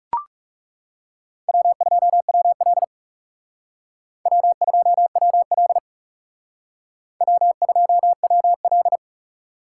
Morse code, heard as a keyed tone, about 38 wpm, 700 Hz. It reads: W2WL W2WL W2WL